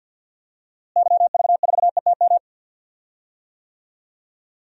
Morse code, heard as a keyed tone, about 49 wpm, 700 Hz. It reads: YV4ETM
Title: DXCC-100 Call Signs - 6 Characters - Repeated 3x - 50wpm